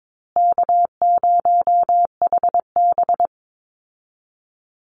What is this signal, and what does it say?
Morse code, heard as a keyed tone, about 22 wpm, 700 Hz. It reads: K0HB
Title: Easy Call Signs Repeated 3x - 22wpm